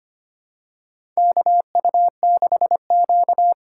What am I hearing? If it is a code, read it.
Morse code, heard as a keyed tone, about 25 wpm, 700 Hz. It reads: KU6Q
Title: USA Call Signs - 4 Characters - 3x Character Spacing - 25wpm